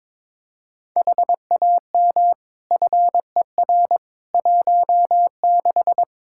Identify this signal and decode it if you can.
Morse code, heard as a keyed tone, about 22 wpm, 700 Hz. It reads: HAM FER 16
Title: K1USN SST QSO Practice 22wpm